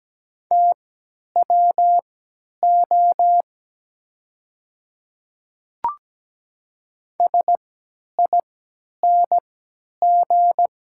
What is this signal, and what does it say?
Morse code, heard as a keyed tone, about 17 wpm, spaced as for 10 wpm, 700 Hz. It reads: TWO SING
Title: 3x Character Spacing - Top 500 Words - 17wpm